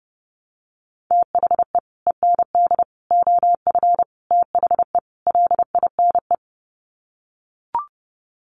Morse code, heard as a keyed tone, about 30 wpm, 700 Hz. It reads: THE END OF THE LINE
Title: Sets of 5 Words Encore2 30wpm